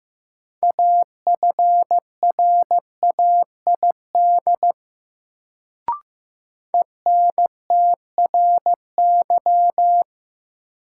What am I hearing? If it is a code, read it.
Morse code, heard as a keyed tone, about 15 wpm, 700 Hz. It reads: AFRAID ENTRY
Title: Sentences from Top 1500 Words - New Word Drill Alt - 15wpm